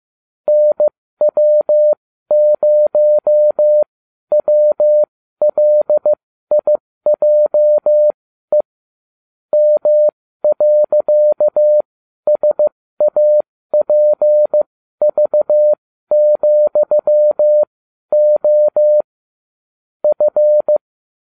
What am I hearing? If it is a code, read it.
Morse code, heard as a keyed tone, about 15 wpm, spaced as for 12 wpm, 600 Hz. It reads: NW0WLIJE M.SAPV,O F